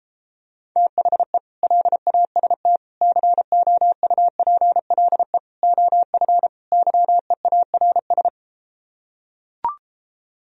Morse code, heard as a keyed tone, about 33 wpm, 700 Hz. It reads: THE LAST COUPLE OF YEARS